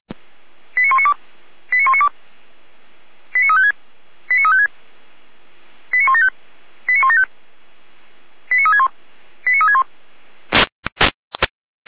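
Five-tone selective call (ZVEI), heard as quick runs of five tones, tones of about 70 ms. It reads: 89192 89192 89356 89356 89175 89175 89361 89361
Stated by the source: ZVEI five-tone call received on an SDR.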